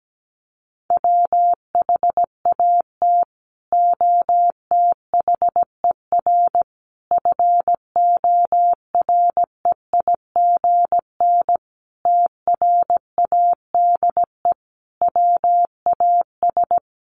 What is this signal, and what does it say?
Morse code, heard as a keyed tone, about 17 wpm, 700 Hz. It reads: WHAT OTHER FOREIGN TRADE WAS
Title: Sentences from Top 1400 Words 17wpm